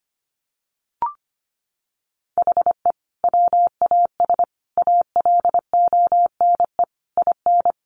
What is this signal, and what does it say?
Morse code, heard as a keyed tone, about 25 wpm, 700 Hz. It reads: HE WAS ALONE IN
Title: Sentences from Top 1000 Words 25wpm